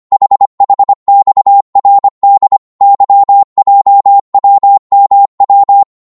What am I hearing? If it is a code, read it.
Morse code, heard as a keyed tone, about 25 wpm, 800 Hz. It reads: HHXRD YJWMW
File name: Morse / ltr-25wpm